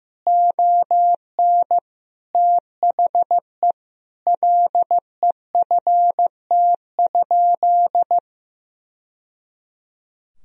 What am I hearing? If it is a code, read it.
Morse code, heard as a keyed tone, about 15 wpm, 700 Hz. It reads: ON THE LEFT?
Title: Sentences from Top 1000 Words - Review Part 2 - 15wpm